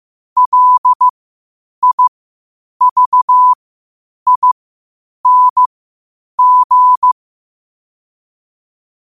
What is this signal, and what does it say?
Morse code, heard as a keyed tone, about 15 wpm, spaced as for 8 wpm, 1000 Hz. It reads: LIVING